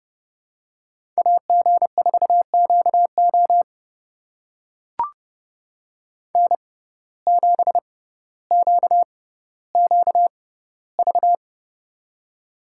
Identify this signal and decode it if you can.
Morse code, heard as a keyed tone, about 30 wpm, spaced as for 15 wpm, 700 Hz. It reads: AG4QO N7QQV
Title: USA Call Signs - 5 Characters - 6x Character Spacing - 30wpm